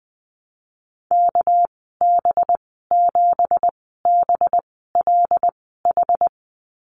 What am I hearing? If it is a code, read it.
Morse code, heard as a keyed tone, about 20 wpm, spaced as for 15 wpm, 700 Hz. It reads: KB7BLH
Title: 200 Lesson 32g - Letter k - Call Signs 20wpm